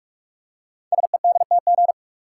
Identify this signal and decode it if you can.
Morse code, heard as a keyed tone, about 44 wpm, 700 Hz. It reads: SEDTG
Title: Code Groups - 5 Letter:Numbers - 45wpm